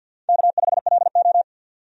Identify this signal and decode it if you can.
Morse code, heard as a keyed tone, about 47 wpm, 700 Hz. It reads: K5LO
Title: USA Call Signs - 4 Characters - 50wpm